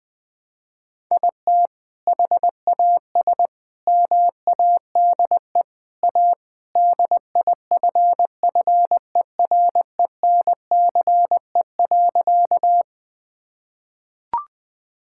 Morse code, heard as a keyed tone, about 20 wpm, 700 Hz. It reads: IT HAS MADE A DIFFERENCE.
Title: Sentences from Top 1000 Words - Review Part 5 - 20wpm